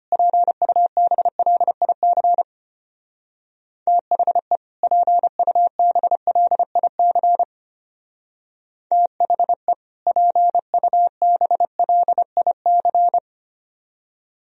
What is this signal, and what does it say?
Morse code, heard as a keyed tone, about 32 wpm, 700 Hz. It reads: PUBLIC THE PUBLIC THE PUBLIC